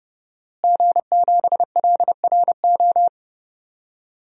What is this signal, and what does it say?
Morse code, heard as a keyed tone, about 32 wpm, 700 Hz. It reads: G7LRO
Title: DXCC-100 Call Signs - 5 Characters - Repeated 3x - 30wpm